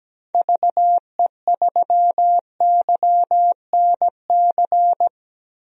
Morse code, heard as a keyed tone, about 17 wpm, 700 Hz. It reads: VE3YNC
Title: DXCC-100 Call Signs - 6 Characters - Repeated 2x - 17wpm